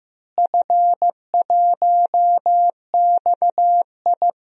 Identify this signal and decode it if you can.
Morse code, heard as a keyed tone, about 15 wpm, 700 Hz. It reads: F1XI
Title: DXCC-100 Call Signs - 4 Characters - Repeated 3x - 15wpm